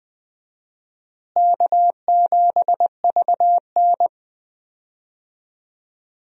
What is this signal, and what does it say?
Morse code, heard as a keyed tone, about 20 wpm, 700 Hz. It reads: K7VN
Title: USA Call Signs - 4 Characters - Repeated 3x - 20wpm